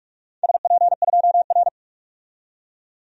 Morse code, heard as a keyed tone, about 44 wpm, 700 Hz. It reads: SP2R
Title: DXCC-100 Call Signs - 4 Characters - 45wpm